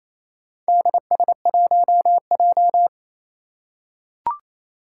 Morse code, heard as a keyed tone, about 27 wpm, 700 Hz. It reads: DS1J
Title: Four Challenging Letter-Number Combinations 28wpm